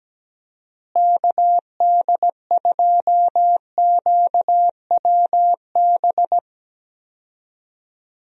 Morse code, heard as a keyed tone, about 17 wpm, 700 Hz. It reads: KD2QWB
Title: USA Call Signs - 6 Characters - Repeated 3x - 17wpm